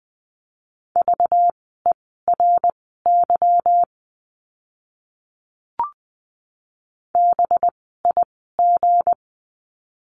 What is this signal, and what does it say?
Morse code, heard as a keyed tone, about 20 wpm, spaced as for 14 wpm, 700 Hz. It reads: VERY BIG
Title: 192 Lesson 31f - Letter v - Words 20wpm